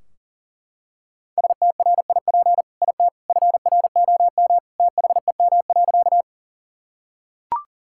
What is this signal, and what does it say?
Morse code, heard as a keyed tone, about 40 wpm, 700 Hz. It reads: STRIP IT FROM THEM.
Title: Sentences from Top 1600 Words 40wpm